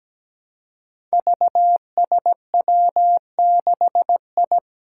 Morse code, heard as a keyed tone, about 17 wpm, 700 Hz. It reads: VSW6I
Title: Five Challenging Letter-Number Combinations 17wpm